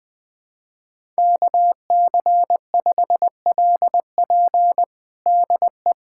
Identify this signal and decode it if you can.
Morse code, heard as a keyed tone, about 21 wpm, 700 Hz. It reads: KC5LP DE
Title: USA Callsign DE Callsign - 5 Characters - 20wpm